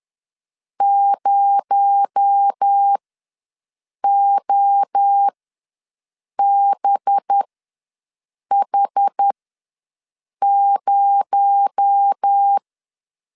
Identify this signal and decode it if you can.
Morse code, heard as a keyed tone, about 10 wpm, spaced as for 6 wpm, 800 Hz. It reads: 0OBH0
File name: LEZIONI / 6 PKX0 / 6 PKX0 ws 10wpm 4220